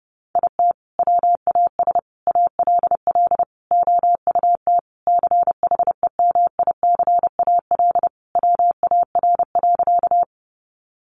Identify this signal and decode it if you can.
Morse code, heard as a keyed tone, about 30 wpm, 700 Hz. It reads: IT WAS ALL OUT CHEMICAL WAR.
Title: Sentences from Top 1200 Words 30wpm